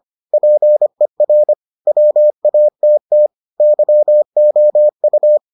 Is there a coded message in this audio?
Yes: Morse code from 0.3 to 5.4 s, heard as a keyed tone, about 25 wpm, 600 Hz. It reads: PER WATT YOU